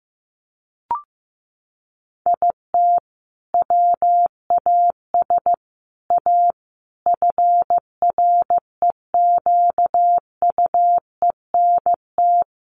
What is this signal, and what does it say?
Morse code, heard as a keyed tone, about 15 wpm, 700 Hz. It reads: IT WAS A FREQUENT
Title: Sentences from Top 1400 Words 15wpm